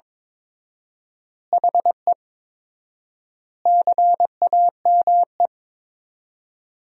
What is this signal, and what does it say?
Morse code, heard as a keyed tone, about 22 wpm, 700 Hz. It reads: HE CAME
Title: 4x Word-Spacing - Sentences from Top 500 Words - 22wpm